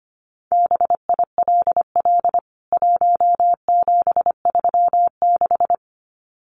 Morse code, heard as a keyed tone, about 25 wpm, 700 Hz. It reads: BILL 1736
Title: CWT Exchange 25wpm